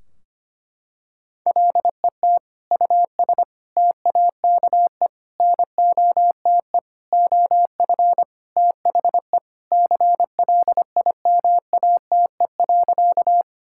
Morse code, heard as a keyed tone, about 25 wpm, 700 Hz. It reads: LET US TAKE NOTE OF THE CLIMATE.
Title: Sentences from Top 1600 Words 25wpm